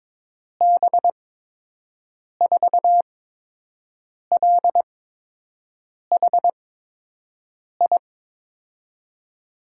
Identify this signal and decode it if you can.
Morse code, heard as a keyed tone, about 22 wpm, spaced as for 6 wpm, 700 Hz. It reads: B4LHI
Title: USA Call Signs - 6 Characters - 8x Character Spacing - 22wpm